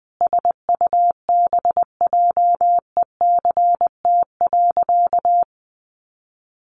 Morse code, heard as a keyed tone, about 20 wpm, 700 Hz. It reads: SUBJECT.
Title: Sentences from Top 800 Words 20wpm